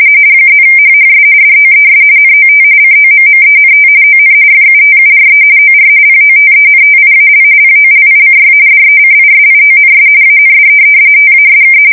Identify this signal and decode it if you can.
RTTY, about 45.45 baud. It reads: +HHS DOES NOT COMMENT ON FUTURE OR POTENTIAL POLICY DECISIONS AND TASK
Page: 170 Shift 2125/2295